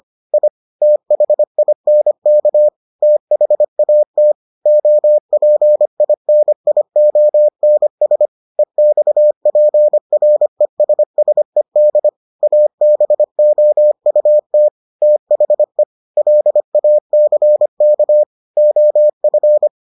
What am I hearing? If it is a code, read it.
Morse code, heard as a keyed tone, about 25 wpm, 600 Hz. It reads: I THINK THAT OPINIONS EXPRESSED ABOUT THE LACK OF